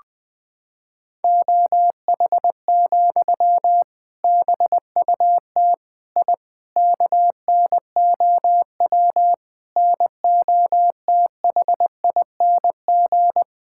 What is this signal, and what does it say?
Morse code, heard as a keyed tone, about 21 wpm, 700 Hz. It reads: OH, BUT I KNOW NOTHING
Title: Sentences from Top 1000 Words - Review Part 6 - 20wpm